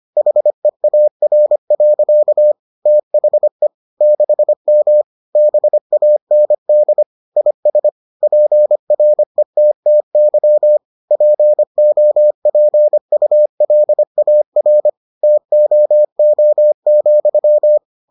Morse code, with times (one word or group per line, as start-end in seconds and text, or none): 0.2-2.5 HEAR.
2.9-3.7 THE
4.0-5.0 6M
5.4-7.0 BAND
7.4-7.9 IS
8.2-10.8 PRETTY
11.1-14.9 POPULAR
15.2-17.8 TOO,